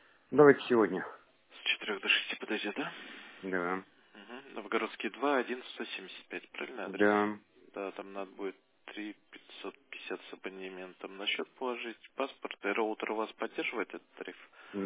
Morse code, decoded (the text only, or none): none